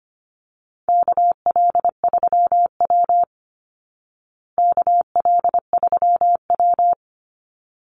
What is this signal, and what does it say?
Morse code, heard as a keyed tone, about 25 wpm, 700 Hz. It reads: KL3W KL3W
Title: Call Signs Repeated 3x - 25wpm